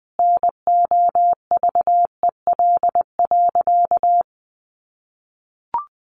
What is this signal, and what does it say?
Morse code, heard as a keyed tone, about 20 wpm, 700 Hz. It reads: NOVEL.
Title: Sentences from Top 1300 Words 20wpm